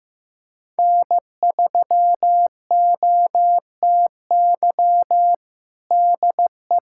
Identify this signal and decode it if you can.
Morse code, heard as a keyed tone, about 15 wpm, 700 Hz. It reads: N3OTY DE
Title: USA Callsign DE Callsign - 5 Characters - 15wpm